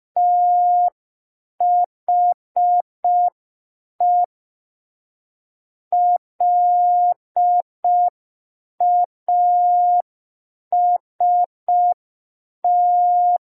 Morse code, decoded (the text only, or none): THE LAST